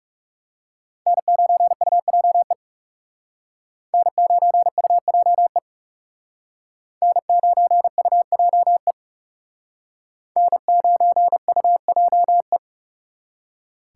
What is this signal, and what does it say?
Morse code, heard as a keyed tone, about 38 wpm, 700 Hz. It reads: N9UJE N9UJE N9UJE N9UJE